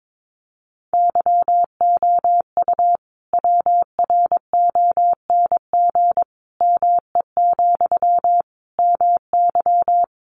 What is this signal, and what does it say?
Morse code, heard as a keyed tone, about 22 wpm, 700 Hz. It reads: YOU WRONG ME, MY
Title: Sentences from Top 900 Words 22wpm